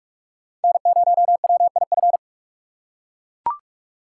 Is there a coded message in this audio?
Yes: Morse code from 0.6 to 2.2 s, heard as a keyed tone, about 43 wpm, 700 Hz. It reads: N0WIF